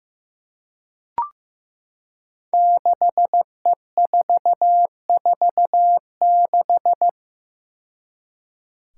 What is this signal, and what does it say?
Morse code, heard as a keyed tone, about 15 wpm, 700 Hz. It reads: 6E446
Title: Five Challenging Letter-Number Combinations 15wpm